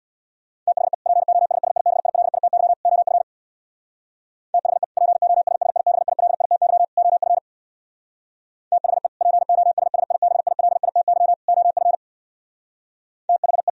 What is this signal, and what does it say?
Morse code, heard as a keyed tone, about 64 wpm, 700 Hz. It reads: THE POSSIBILITY OF THE POSSIBILITY OF THE POSSIBILITY OF THE